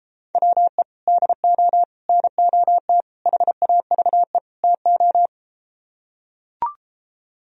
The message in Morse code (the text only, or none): WE DO NOT HAVE TO